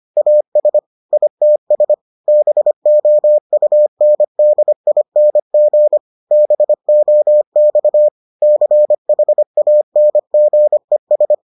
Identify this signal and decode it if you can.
Morse code, heard as a keyed tone, about 25 wpm, 600 Hz. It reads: AS ITS BOUNDING BOX CHANGES